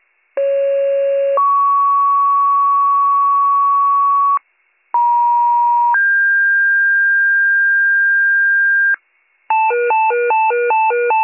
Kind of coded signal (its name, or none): two-tone page